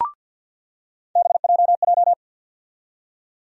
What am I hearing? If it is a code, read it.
Morse code, heard as a keyed tone, about 48 wpm, 700 Hz. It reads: BJJ
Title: USA Call Sign Postfixes - 3 Letters 50wpm